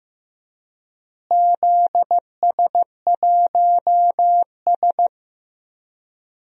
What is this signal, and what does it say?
Morse code, heard as a keyed tone, about 15 wpm, 700 Hz. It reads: ZS1S
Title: DXCC-100 Call Signs - 4 Characters - Repeated 2x - 15wpm